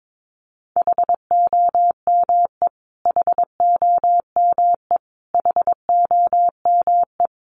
Morse code, heard as a keyed tone, about 22 wpm, 700 Hz. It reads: HOME HOME HOME
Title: Top 500 Words Repeated 22wpm